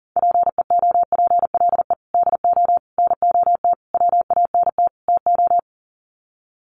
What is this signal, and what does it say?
Morse code, heard as a keyed tone, about 40 wpm, 700 Hz. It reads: PEOPLE DO NOT WANT TO
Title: Sets of 5 Words Encore2 40wpm